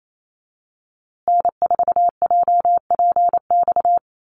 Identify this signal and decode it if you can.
Morse code, heard as a keyed tone, about 28 wpm, 700 Hz. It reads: N4JPX
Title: USA Call Signs Repeated 3x 28wpm